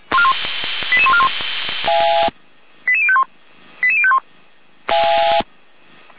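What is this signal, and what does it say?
Five-tone selective call (ZVEI), heard as quick runs of five tones, tones of about 70 ms. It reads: 88141 88051 88051
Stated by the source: Five-tone ZVEI selective call received on an SDR.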